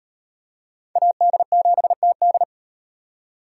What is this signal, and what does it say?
Morse code, heard as a keyed tone, about 36 wpm, 700 Hz. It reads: AD7TD